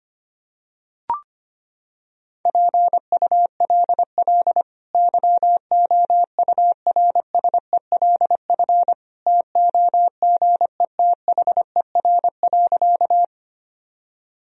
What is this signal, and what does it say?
Morse code, heard as a keyed tone, about 25 wpm, 700 Hz. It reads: PULL YOURSELF TOGETHER.
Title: Sentences from Top 1000 Words - Review Part 2 - 25wpm